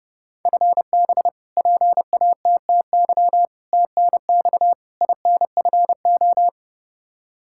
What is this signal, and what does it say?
Morse code, heard as a keyed tone, about 32 wpm, 700 Hz. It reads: FB PATTY TNX INFO
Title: K1USN SST QSO Practice 30wpm